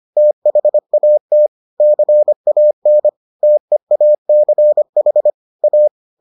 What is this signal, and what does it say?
Morse code, heard as a keyed tone, about 25 wpm, 600 Hz. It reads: THAT CAN TEACH A